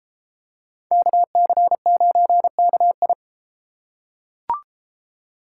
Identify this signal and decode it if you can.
Morse code, heard as a keyed tone, about 32 wpm, 700 Hz. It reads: KC9KI